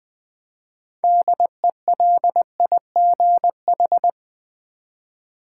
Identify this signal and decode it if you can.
Morse code, heard as a keyed tone, about 21 wpm, 700 Hz. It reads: DELIGH
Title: Build A Word - 3 Letters At A Time - 9-Letter Words - 20wpm